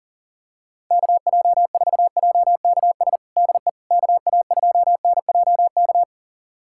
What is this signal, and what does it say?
Morse code, heard as a keyed tone, about 40 wpm, 700 Hz. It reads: KJ4JKS DE KA2NJK
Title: USA Callsign DE Callsign - 6 Characters - 40wpm